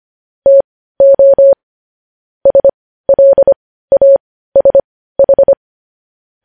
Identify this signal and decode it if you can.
Morse code, heard as a keyed tone, about 25 wpm, spaced as for 15 wpm, 550 Hz. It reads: TO SLASH
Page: news-headlines-in-morse-code-at-15-wpm